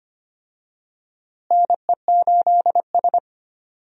Morse code, heard as a keyed tone, about 25 wpm, 700 Hz. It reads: NE8S
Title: USA Call Signs - 4 Characters - Repeated 3x - 25wpm